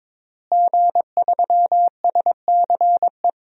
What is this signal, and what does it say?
Morse code, heard as a keyed tone, about 22 wpm, 700 Hz. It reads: G3SCE
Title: DXCC-100 Call Signs - 5 Characters - Repeated 2x - 22wpm